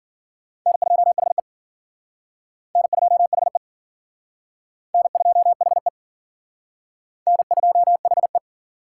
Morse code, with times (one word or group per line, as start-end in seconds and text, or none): 0.7-1.4 N2HE
2.8-3.6 N2HE
4.9-5.9 N2HE
7.3-8.4 N2HE